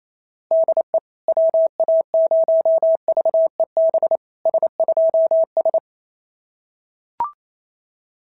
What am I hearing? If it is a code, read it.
Morse code, heard as a keyed tone, about 27 wpm, 690 Hz. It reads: DE WA0VEB S2S
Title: SOTA QSO Practice 28wpm